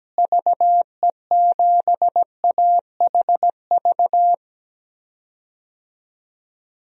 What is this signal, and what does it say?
Morse code, heard as a keyed tone, about 17 wpm, 700 Hz. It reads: VE7AHV
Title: DXCC-100 Call Signs - 6 Characters - Repeated 2x - 17wpm